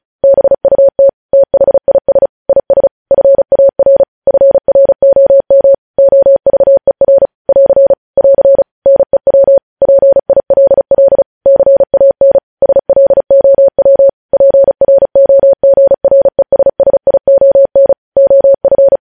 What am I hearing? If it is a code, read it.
Morse code, heard as a keyed tone, about 36 wpm, 550 Hz. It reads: BUT THIS IS FAR FROM OVER <AR> <AR> NEW PILL CAN SLOW PROGRESSION OF
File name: podcast / News headlines-1x-35x35